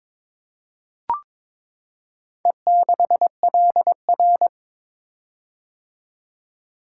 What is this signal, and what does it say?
Morse code, heard as a keyed tone, about 22 wpm, 700 Hz. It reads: E6LR
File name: practice / 22wpm / Four Letter-Number Combinations - 22wpm